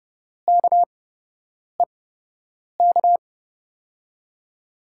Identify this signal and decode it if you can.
Morse code, heard as a keyed tone, about 31 wpm, spaced as for 8 wpm, 700 Hz. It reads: KEK
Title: USA Call Signs - 5 Characters - 8x Character Spacing - 30wpm